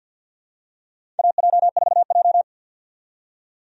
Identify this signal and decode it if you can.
Morse code, heard as a keyed tone, about 47 wpm, 700 Hz. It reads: AJ4J
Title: USA Call Signs - 4 Characters - 50wpm